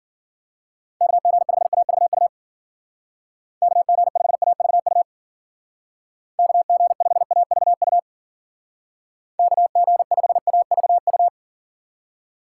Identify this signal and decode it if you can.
Morse code, heard as a keyed tone, about 50 wpm, 700 Hz. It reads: KG5AVU KG5AVU KG5AVU KG5AVU
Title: Speed Racing - USA 2x3 Call Signs 40wpm